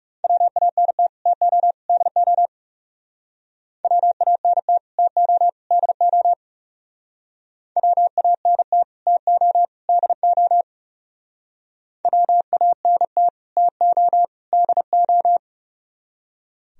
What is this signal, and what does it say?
Morse code, heard as a keyed tone, about 39 wpm, 700 Hz. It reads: WANT TO DO WANT TO DO WANT TO DO WANT TO DO